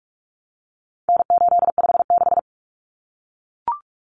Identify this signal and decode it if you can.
Morse code, heard as a keyed tone, about 44 wpm, 700 Hz. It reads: N856
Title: Common Cut Numbers - 4-Digit 45wpm